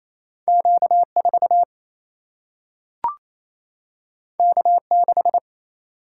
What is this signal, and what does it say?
Morse code, heard as a keyed tone, about 28 wpm, 700 Hz. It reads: Q4 K6
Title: Single-Letter Single-Number 28wpm